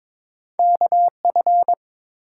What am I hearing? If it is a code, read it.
Morse code, heard as a keyed tone, about 22 wpm, 700 Hz. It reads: KF
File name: practice / 22wpm / Two Letter-Number Combinations - 22wpm